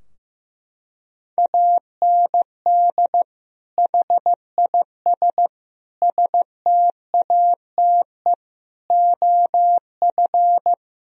Morse code, heard as a keyed tone, about 15 wpm, 700 Hz. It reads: AND HIS STATE OF
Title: Sentences from Top 1000 Words - Review Part 2 - 15wpm